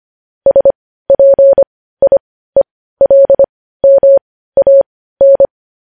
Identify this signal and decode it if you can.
Morse code, heard as a keyed tone, about 25 wpm, spaced as for 15 wpm, 550 Hz. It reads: SPIELMAN
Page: News Headlines in Morse Code at 15 WPM